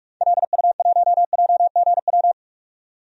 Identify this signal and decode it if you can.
Morse code, heard as a keyed tone, about 47 wpm, 700 Hz. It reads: LU1JGW
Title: DXCC-100 Call Signs - 6 Characters - 45wpm